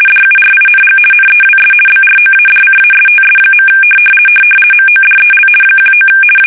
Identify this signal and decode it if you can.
RTTY, about 50 baud. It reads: DURING NEWS BROADCASTS. MODEL 19
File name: fsk850-50B / teletype-corp